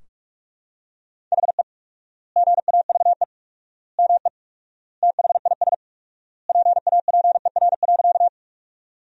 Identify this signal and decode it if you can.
Morse code, heard as a keyed tone, about 46 wpm, 700 Hz. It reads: HE GAVE ME THIS PAPER.
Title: 4x Word-Spacing - Sentences from Top 500 Words - 45wpm